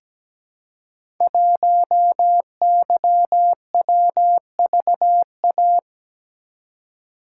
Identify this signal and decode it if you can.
Morse code, heard as a keyed tone, about 17 wpm, 700 Hz. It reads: 1YWVA
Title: Code Groups - 5 Letter:Numbers - 17wpm